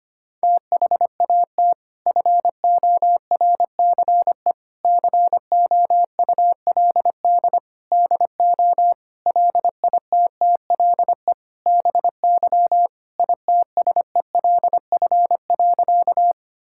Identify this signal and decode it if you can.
Morse code, heard as a keyed tone, about 25 wpm, 700 Hz. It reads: THAT FORCE COULD DO LITTLE BY ITSELF.
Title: Sentences from Top 1000 Words - Review Part 5 - 25wpm